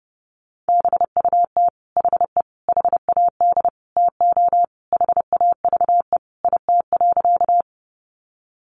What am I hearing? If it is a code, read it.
Morse code, heard as a keyed tone, about 30 wpm, 700 Hz. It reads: BUT HE HAD TO HAVE IT.
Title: Sentences from Top 100 Words 30wpm